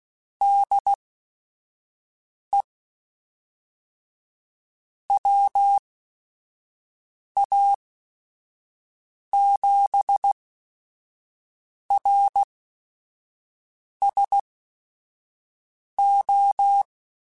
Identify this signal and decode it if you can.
Morse code, heard as a keyed tone, about 16 wpm, spaced as for 5 wpm, 800 Hz. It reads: DE WA7RSO